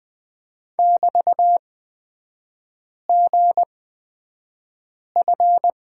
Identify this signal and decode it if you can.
Morse code, heard as a keyed tone, about 20 wpm, spaced as for 5 wpm, 700 Hz. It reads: <BT>GF
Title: 224 Lesson 36e - BT - Three Characters 20wpm